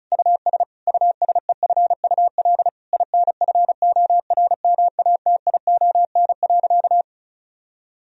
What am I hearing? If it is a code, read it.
Morse code, heard as a keyed tone, about 36 wpm, 700 Hz. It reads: US USEFUL INFORMATION.
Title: Sentences from Top 1000 Words - Review Part 4 - 35wpm